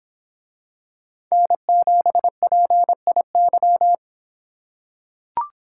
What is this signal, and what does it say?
Morse code, heard as a keyed tone, about 26 wpm, 700 Hz. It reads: N7PIY